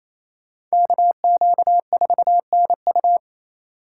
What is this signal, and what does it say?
Morse code, heard as a keyed tone, about 34 wpm, 700 Hz. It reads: KQ4NU
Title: Speed Racing - USA Call Signs 28wpm